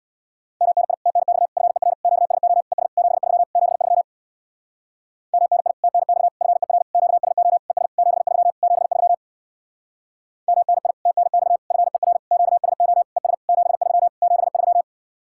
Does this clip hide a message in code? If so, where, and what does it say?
Yes: Morse code from 0.6 to 14.8 s, heard as a keyed tone, about 66 wpm, 700 Hz. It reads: MNI TNX FER QSO ES 73 73 MNI TNX FER QSO ES 73 73 MNI TNX FER QSO ES 73 73